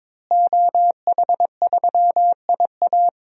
Morse code, heard as a keyed tone, about 22 wpm, 700 Hz. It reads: OH3IA
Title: DXCC-100 Call Signs - 5 Characters - Repeated 3x - 22wpm